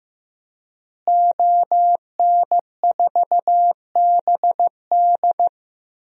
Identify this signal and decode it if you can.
Morse code, heard as a keyed tone, about 15 wpm, 700 Hz. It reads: ON4BD
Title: DXCC-100 Call Signs - 5 Characters - Repeated 2x - 15wpm